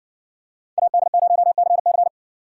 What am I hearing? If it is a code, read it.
Morse code, heard as a keyed tone, about 60 wpm, 700 Hz. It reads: AD0CC